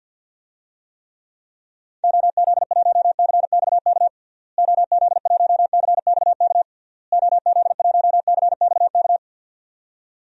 Morse code, heard as a keyed tone, about 50 wpm, 700 Hz. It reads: OZ1CXK OZ1CXK OZ1CXK
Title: DXCC-100 Call Signs - 6 Characters - Repeated 3x - 50wpm